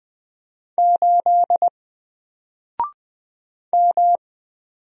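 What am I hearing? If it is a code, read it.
Morse code, heard as a keyed tone, about 20 wpm, 700 Hz. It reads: 8 M
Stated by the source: Single letter and number Morse drill at 20 wpm, from N4LCD.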